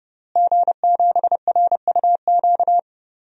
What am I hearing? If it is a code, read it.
Morse code, heard as a keyed tone, about 30 wpm, 700 Hz. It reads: G7RUQ
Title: DXCC-100 Call Signs - 5 Characters - Repeated 2x - 30wpm